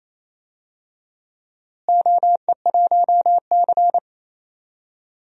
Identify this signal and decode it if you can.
Morse code, heard as a keyed tone, about 28 wpm, 700 Hz. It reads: OE1C
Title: DXCC-100 Call Signs - 4 Characters - Repeated 2x - 28wpm